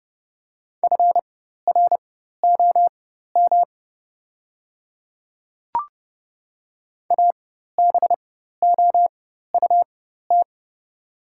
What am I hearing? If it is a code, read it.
Morse code, heard as a keyed tone, about 29 wpm, spaced as for 14 wpm, 690 Hz. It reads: FROM ABOUT
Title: 4x Character Spacing - Top 100 Words - 30wpm